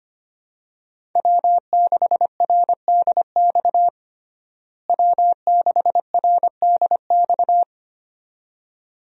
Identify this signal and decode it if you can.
Morse code, heard as a keyed tone, about 25 wpm, 700 Hz. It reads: W6RDX W6RDX
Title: USA Call Signs - 5 Characters - Repeated 2x - 25wpm